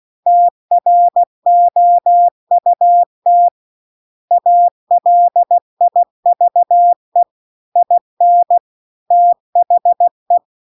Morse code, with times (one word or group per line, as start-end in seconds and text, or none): 0.3-3.5 TROUT
4.3-7.2 ALIVE
7.8-8.6 IN
9.1-10.4 THE